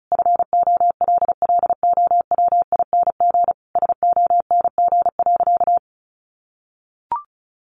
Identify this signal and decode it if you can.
Morse code, heard as a keyed tone, about 34 wpm, 700 Hz. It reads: FOLLOWING SONG.
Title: Sentences from Top 900 Words 35wpm